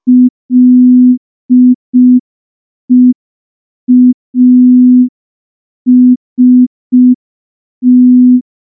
Morse code, heard as a keyed tone, about 5 wpm, 250 Hz. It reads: LEAST
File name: morse / morse-training-05WPM